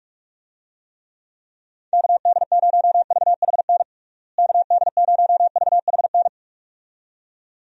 Morse code, heard as a keyed tone, about 47 wpm, 700 Hz. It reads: KD0VHN KD0VHN
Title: USA Call Signs - 6 Characters - Repeated 2x - 45wpm